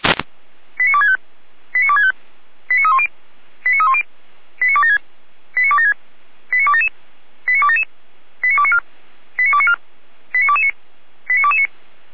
Five-tone selective call (ZVEI), heard as quick runs of five tones, tones of about 80 ms. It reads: 89265 89265 89310 89310 89276 89276 89277 89277 89294 89294 89229 89229